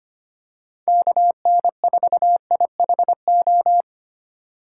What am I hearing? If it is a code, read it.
Morse code, heard as a keyed tone, about 25 wpm, 700 Hz. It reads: KN4IHO
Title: USA Call Signs - 6 Characters - 6x Character Spacing - 25wpm